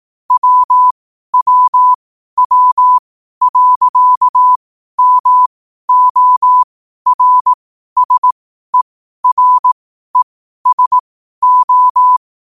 Morse code, heard as a keyed tone, about 18 wpm, spaced as for 12 wpm, 1000 Hz. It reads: WWW.MORSERESO
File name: qotdfiles / 12wpm